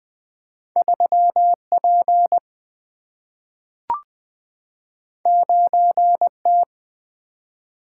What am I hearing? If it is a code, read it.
Morse code, heard as a keyed tone, about 20 wpm, 700 Hz. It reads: 3 P 9 T
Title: Single-Letter Single-Number 20wpm